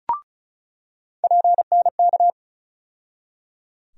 Morse code, heard as a keyed tone, about 36 wpm, 700 Hz. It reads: PNK
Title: USA Call Sign Postfixes - 3 Letters 35wpm